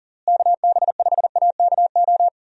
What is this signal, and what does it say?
Morse code, heard as a keyed tone, about 37 wpm, 700 Hz. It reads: KB5AKO
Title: USA Call Signs - 6 Characters - Repeated 2x - 40wpm